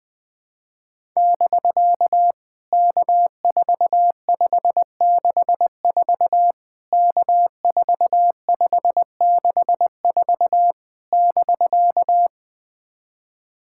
Morse code, heard as a keyed tone, about 21 wpm, 700 Hz. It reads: <BK> K4564 K4564 <BK>
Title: POTA QSO Practice - 20wpm